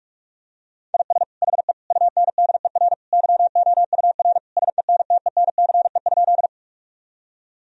Morse code, heard as a keyed tone, about 46 wpm, 700 Hz. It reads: IS HE UNDER YOUR SENTENCE?
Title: Sentences from Top 1000 Words - Review Part 1 - 45wpm